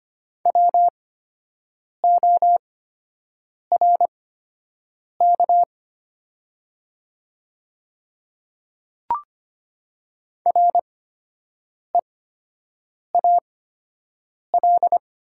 Morse code, heard as a keyed tone, about 25 wpm, spaced as for 7 wpm, 700 Hz. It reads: WORK REAL
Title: 8x Character Spacing - Top 500 Words - 25wpm